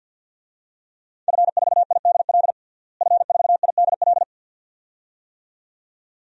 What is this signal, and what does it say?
Morse code, heard as a keyed tone, about 51 wpm, 700 Hz. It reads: F4IDL F4IDL
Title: DXCC-100 Call Signs - 5 Characters - Repeated 2x - 50wpm